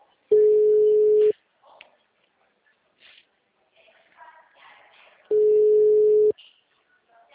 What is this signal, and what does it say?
Ringback tone (Europe, 425 Hz), with speech.